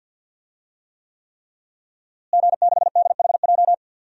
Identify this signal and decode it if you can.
Morse code, heard as a keyed tone, about 51 wpm, 700 Hz. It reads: G6DHJ
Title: DXCC-100 Call Signs - 5 Characters - Repeated 3x - 50wpm